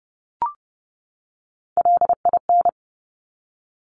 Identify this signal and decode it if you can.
Morse code, heard as a keyed tone, about 30 wpm, 700 Hz. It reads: LIN
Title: Top Three Letter Combinations in Words 30wpm